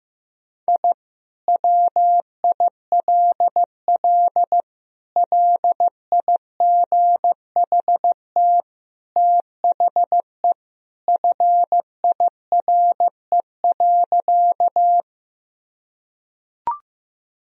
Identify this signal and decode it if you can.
Morse code, heard as a keyed tone, about 15 wpm, 700 Hz. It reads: I WILL LIGHT THE FIRE.
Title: Sentences from Top 1000 Words - Review Part 1 - 15wpm